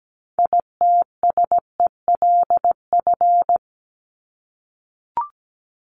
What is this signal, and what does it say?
Morse code, heard as a keyed tone, about 17 wpm, 700 Hz. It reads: ITSELF
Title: Top 501-750 Words 17wpm